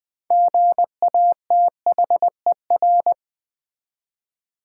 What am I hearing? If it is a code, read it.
Morse code, heard as a keyed tone, about 20 wpm, 700 Hz. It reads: GATHER
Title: Six Letter Words - 20wpm